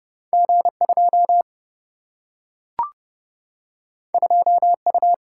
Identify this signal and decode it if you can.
Morse code, heard as a keyed tone, about 30 wpm, 700 Hz. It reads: G2 2U
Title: Single-Letter Single-Number 30wpm